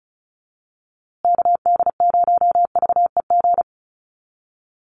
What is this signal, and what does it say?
Morse code, heard as a keyed tone, about 34 wpm, 700 Hz. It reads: KD0VEG
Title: USA Call Signs Repeated 2x 35wpm